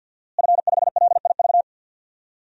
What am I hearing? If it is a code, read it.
Morse code, heard as a keyed tone, about 51 wpm, 700 Hz. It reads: F5LIV
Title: DXCC-100 Call Signs - 5 Characters - Repeated 3x - 50wpm